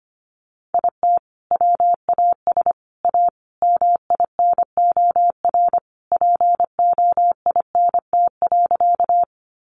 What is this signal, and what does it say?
Morse code, heard as a keyed tone, about 25 wpm, 700 Hz. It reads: IT WAS A MINOR POINT.
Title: Sentences from Top 1500 Words 25wpm